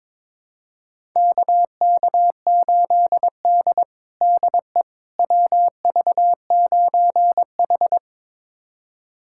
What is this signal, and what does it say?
Morse code, heard as a keyed tone, about 22 wpm, 700 Hz. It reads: KK8D DE WV9H
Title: USA Callsign DE Callsign - 4 Characters - 22wpm